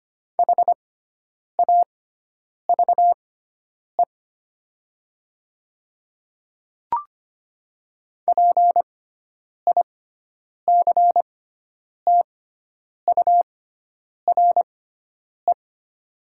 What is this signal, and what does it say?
Morse code, heard as a keyed tone, about 25 wpm, spaced as for 9 wpm, 700 Hz. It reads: HAVE PICTURE
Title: 6x Character Spacing - Top 300 Words - 25wpm